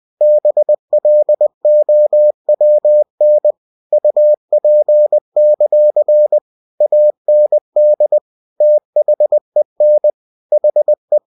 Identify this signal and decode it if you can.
Morse code, heard as a keyed tone, about 20 wpm, 600 Hz. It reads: BLOWN UP; AND THEN HE